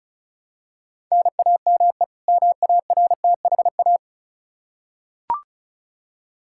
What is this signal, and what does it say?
Morse code, heard as a keyed tone, about 34 wpm, 700 Hz. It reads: NAME MARTHA
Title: Practice QSO - Exchange 1 - Beginning - 35wpm